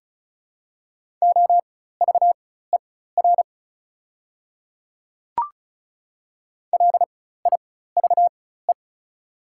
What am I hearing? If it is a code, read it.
Morse code, heard as a keyed tone, about 36 wpm, spaced as for 16 wpm, 700 Hz. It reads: OVER LIVE
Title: 4x Character Spacing - Top 200 Words - 35wpm